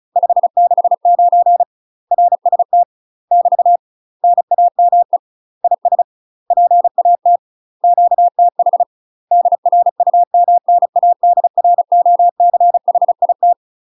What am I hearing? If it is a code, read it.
Morse code, heard as a keyed tone, about 36 wpm, 700 Hz. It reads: 569 RST <BT> NAME IS PAT QTH DRUMNADROCHIT
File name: UK MASTER QSOs TRAINING / 35WPM / UK MASTER QSOs 0007